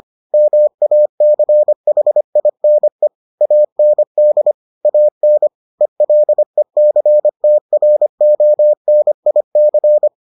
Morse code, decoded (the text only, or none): MACHINE AND AN ELECTRONIC